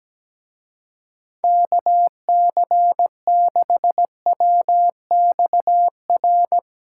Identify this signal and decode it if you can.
Morse code, heard as a keyed tone, about 17 wpm, 700 Hz. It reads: KC6WXR
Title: Speed Racing - USA Call Signs 17wpm